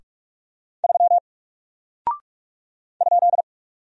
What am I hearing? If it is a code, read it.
Morse code, heard as a keyed tone, about 44 wpm, 700 Hz. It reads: 3 ?
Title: Single Character (Letter:Number:Punctuation:Prosign) Alt 45wpm